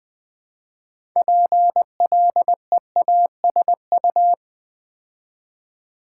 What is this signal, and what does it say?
Morse code, heard as a keyed tone, about 20 wpm, 700 Hz. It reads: PLEASU
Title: Build A Word - 2 Letters At A Time - 8-Letter Words - 20wpm